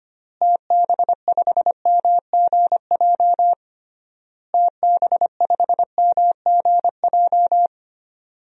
Morse code, heard as a keyed tone, about 25 wpm, 700 Hz. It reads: TB5MGJ TB5MGJ
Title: DXCC-100 Call Signs - 6 Characters - Repeated 3x - 25wpm